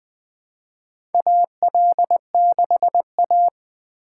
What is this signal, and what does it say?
Morse code, heard as a keyed tone, about 21 wpm, 700 Hz. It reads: AL6A
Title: USA Call Signs - 4 Characters - 2x Character Spacing - 20wpm